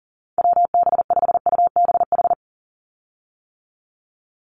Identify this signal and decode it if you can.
Morse code, heard as a keyed tone, about 40 wpm, 690 Hz. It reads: WB5UBH
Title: USA Call Signs 40wpm